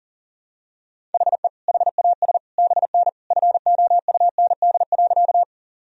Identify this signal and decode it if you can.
Morse code, heard as a keyed tone, about 40 wpm, 700 Hz. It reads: HE HAS BN FOUND.
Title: Sentences from Top 300 Words and Abbreviations 40wpm